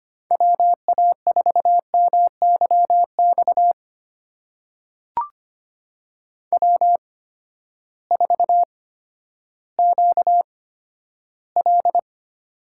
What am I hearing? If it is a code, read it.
Morse code, heard as a keyed tone, about 25 wpm, spaced as for 12 wpm, 700 Hz. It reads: WA4MYX W4QL
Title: USA Call Signs - 8x Character Spacing - 25wpm